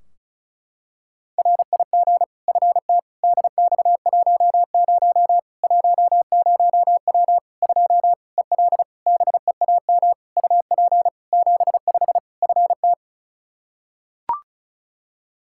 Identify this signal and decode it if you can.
Morse code, heard as a keyed tone, about 36 wpm, 700 Hz. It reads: RIG FT DX10 10W 2 EL BEAM UP 75 FT
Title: K1USN SST QSO Practice 35wpm